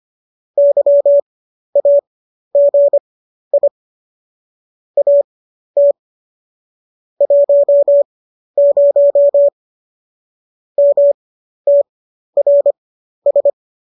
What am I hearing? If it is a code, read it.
Morse code, heard as a keyed tone, about 25 wpm, spaced as for 12 wpm, 570 Hz. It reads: YAGI AT 10 MTRS